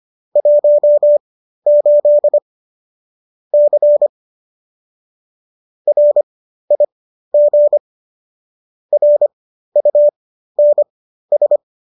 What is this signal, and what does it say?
Morse code, heard as a keyed tone, about 25 wpm, spaced as for 13 wpm, 590 Hz. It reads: 18 C RIG RUNS
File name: qso214 13